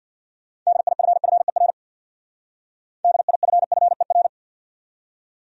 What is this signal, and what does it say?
Morse code, heard as a keyed tone, about 62 wpm, 700 Hz. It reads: DIFFER DIFFER
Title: Speed-Racing Top 500 Words 50wpm